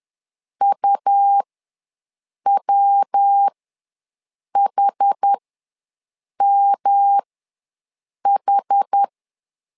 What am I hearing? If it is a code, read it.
Morse code, heard as a keyed tone, about 10 wpm, spaced as for 6 wpm, 800 Hz. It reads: UWHMH
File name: LEZIONI / 3 MWRO / 3 MW ws 10wpm 2213